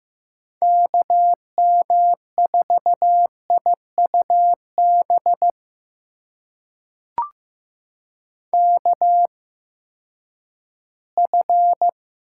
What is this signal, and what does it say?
Morse code, heard as a keyed tone, about 15 wpm, spaced as for 10 wpm, 700 Hz. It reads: KM4IUB KF
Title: USA Call Signs - 6 Characters - 8x Character Spacing - 15wpm